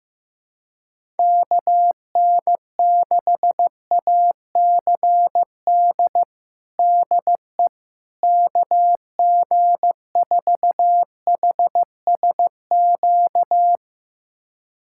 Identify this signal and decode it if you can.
Morse code, heard as a keyed tone, about 15 wpm, 700 Hz. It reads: KN6ACD DE KG4HSQ
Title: USA Callsign DE Callsign 15wpm